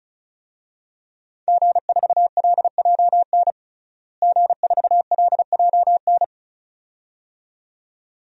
Morse code, heard as a keyed tone, about 36 wpm, 700 Hz. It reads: G4LJN G4LJN
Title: DXCC-100 Call Signs - 5 Characters - Repeated 2x - 35wpm